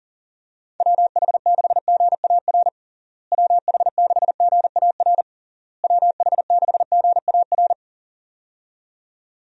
Morse code, heard as a keyed tone, about 40 wpm, 700 Hz. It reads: WH6GAR WH6GAR WH6GAR
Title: USA Call Signs - 6 Characters - Repeated 3x - 40wpm